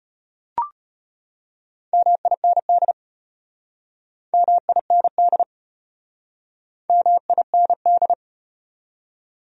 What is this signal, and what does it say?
Morse code, heard as a keyed tone, about 33 wpm, 700 Hz. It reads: MIND MIND MIND